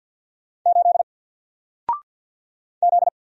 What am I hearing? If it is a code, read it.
Morse code, heard as a keyed tone, about 48 wpm, 700 Hz. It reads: 8 Z
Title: Single Letter-Number Alt 50wpm